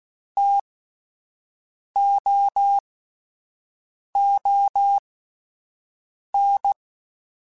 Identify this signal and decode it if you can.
Morse code, heard as a keyed tone, about 16 wpm, spaced as for 5 wpm, 800 Hz. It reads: TOON
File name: Alphabetical / Alphabetical mp3s / Ts @5